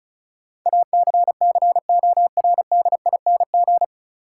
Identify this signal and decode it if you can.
Morse code, heard as a keyed tone, about 36 wpm, 700 Hz. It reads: ACCORDING
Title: Sentences from Top 700 Words - New Word Drill - 35wpm